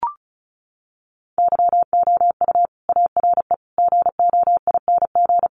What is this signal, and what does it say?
Morse code, heard as a keyed tone, about 34 wpm, 700 Hz. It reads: YOU ARE GOING
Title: Sets of 4 Words 35wpm